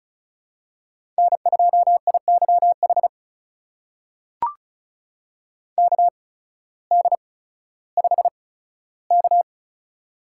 Morse code, heard as a keyed tone, about 36 wpm, spaced as for 16 wpm, 700 Hz. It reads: N2IYH KD5K
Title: USA Call Signs - 8x Character Spacing - 35wpm